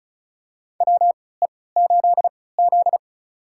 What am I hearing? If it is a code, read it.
Morse code, heard as a keyed tone, about 36 wpm, spaced as for 20 wpm, 700 Hz. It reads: WE8Z
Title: USA Call Signs - 4 Characters - 3x Character Spacing - 35wpm